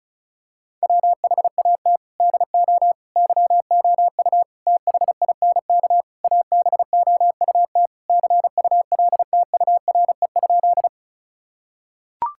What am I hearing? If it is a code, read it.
Morse code, heard as a keyed tone, about 36 wpm, 700 Hz. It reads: WHAT DO YOU THINK ABOUT CULTURE?
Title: Sentences from Top 1000 Words - Review Part 6 - 35wpm